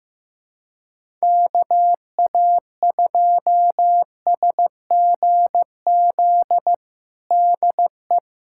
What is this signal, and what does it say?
Morse code, heard as a keyed tone, about 15 wpm, 700 Hz. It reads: KA2SGZ DE
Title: USA Callsign DE Callsign 15wpm